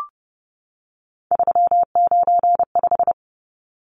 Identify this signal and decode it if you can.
Morse code, heard as a keyed tone, about 30 wpm, 700 Hz. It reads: 395
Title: Numbers - 3-Digit 30wpm